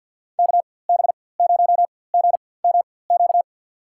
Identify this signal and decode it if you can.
Morse code, heard as a keyed tone, about 52 wpm, spaced as for 24 wpm, 700 Hz. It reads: KB0GMQ
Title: USA Call Signs - 4x Character Spacing - 50wpm